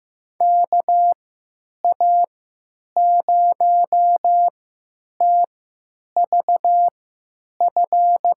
Morse code, heard as a keyed tone, about 15 wpm, spaced as for 9 wpm, 700 Hz. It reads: KA0TVF
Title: USA Call Signs - 3x Character Spacing - 15wpm